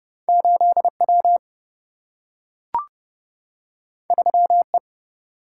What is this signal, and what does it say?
Morse code, heard as a keyed tone, about 30 wpm, 700 Hz. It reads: 8W 3E
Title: Single-Letter Single-Number 30wpm